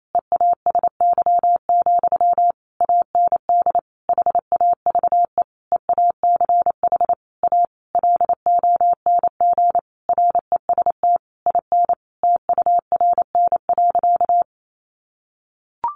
Morse code, heard as a keyed tone, about 28 wpm, 700 Hz. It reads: EASY, AND HAVE EACH A LONG REST IN TURN.
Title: Sentences from Top 700 Words 28wpm